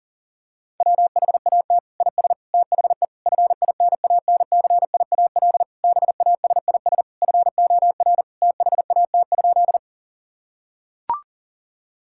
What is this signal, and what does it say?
Morse code, heard as a keyed tone, about 40 wpm, 700 Hz. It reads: WHAT IS THE FINANCIAL BASIS FOR THAT?
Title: Sentences from Top 1000 Words - Review Part 5 - 40wpm